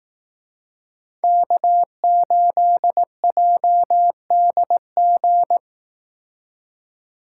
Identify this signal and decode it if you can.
Morse code, heard as a keyed tone, about 18 wpm, 700 Hz. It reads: K8JDG
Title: Speed Racing - USA 1x3 Call Signs 15wpm